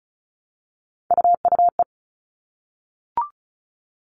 Morse code, heard as a keyed tone, about 34 wpm, 700 Hz. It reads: UUE
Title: All Cut Numbers - 3-Digit 35wpm